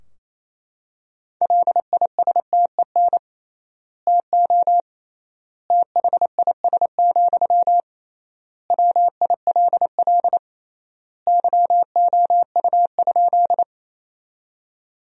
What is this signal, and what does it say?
Morse code, heard as a keyed tone, about 28 wpm, 700 Hz. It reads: LISTEN TO THIS, WILL YOU?
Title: 3x Word-Spacing - Sentences from Top 400 Words - 28wpm